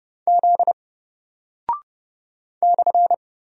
Morse code, heard as a keyed tone, about 30 wpm, 700 Hz. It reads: Z /
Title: Single Character (Letter:Number:Punctuation:Prosign) Alt 30wpm